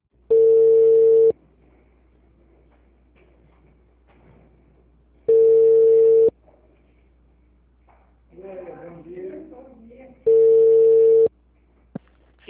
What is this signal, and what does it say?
Ringback tone (China, 450 Hz), with speech.